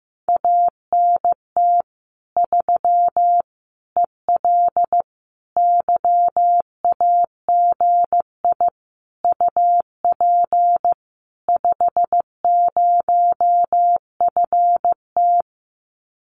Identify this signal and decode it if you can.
Morse code, heard as a keyed tone, about 15 wpm, 700 Hz. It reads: ANT 3 EL YAGI UP 50FT
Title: Common QSO Phrases Encore 15wpm